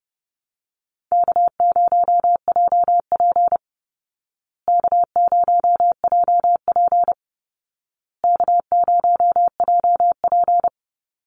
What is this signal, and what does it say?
Morse code, heard as a keyed tone, about 30 wpm, 700 Hz. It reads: K0JP K0JP K0JP
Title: Easy Call Signs Repeated 3x - 30wpm